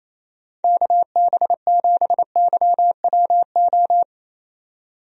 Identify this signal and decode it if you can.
Morse code, heard as a keyed tone, about 27 wpm, 700 Hz. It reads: KB7YWO
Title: USA Call Signs - 6 Characters - 28wpm